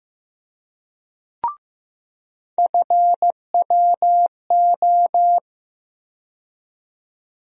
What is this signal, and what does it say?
Morse code, heard as a keyed tone, about 15 wpm, 700 Hz. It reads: FWO
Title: USA Call Sign Postfixes - 3 Letters 15wpm